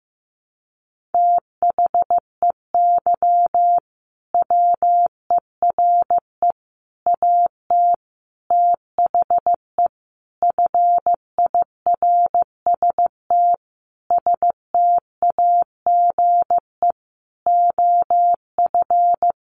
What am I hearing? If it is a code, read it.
Morse code, heard as a keyed tone, about 15 wpm, 700 Hz. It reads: THEY WERE AT THE FIRST STAGE OF
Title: Sentences from Top 1200 Words 15wpm